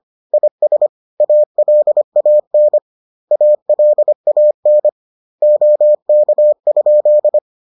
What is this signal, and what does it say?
Morse code, heard as a keyed tone, about 25 wpm, 600 Hz. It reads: IS ALAN ALAN OK?